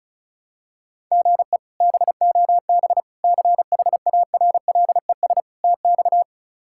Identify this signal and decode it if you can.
Morse code, heard as a keyed tone, about 36 wpm, 700 Hz. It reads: GE BOB CHARLES TX
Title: K1USN SST QSO Practice - 35wpm